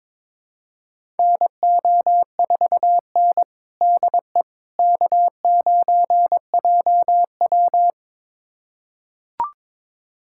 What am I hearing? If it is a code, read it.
Morse code, heard as a keyed tone, about 22 wpm, 700 Hz. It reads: NO4N DE K9JW
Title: USA Callsign DE Callsign - 4 Characters - 22wpm